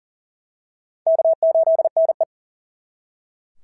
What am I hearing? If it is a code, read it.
Morse code, heard as a keyed tone, about 40 wpm, 690 Hz. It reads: K8NE
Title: SOTA QSO Practice 40wpm